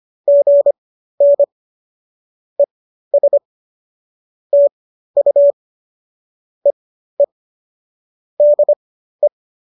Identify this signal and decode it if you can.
Morse code, heard as a keyed tone, about 25 wpm, spaced as for 13 wpm, 590 Hz. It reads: GN ES TU EE DE